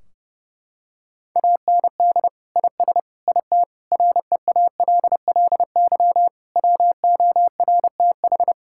Morse code, heard as a keyed tone, about 30 wpm, 700 Hz. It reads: AND IS IT REALLY WORTH
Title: Sentences from Top 1600 Words 30wpm